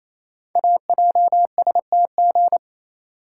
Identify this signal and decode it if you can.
Morse code, heard as a keyed tone, about 27 wpm, 700 Hz. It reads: AJSTG
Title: Code Groups - 5 Letters - 28wpm